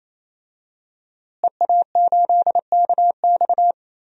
Morse code, heard as a keyed tone, about 27 wpm, 700 Hz. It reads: EA8KX
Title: DXCC-100 Call Signs - 5 Characters - Repeated 3x - 28wpm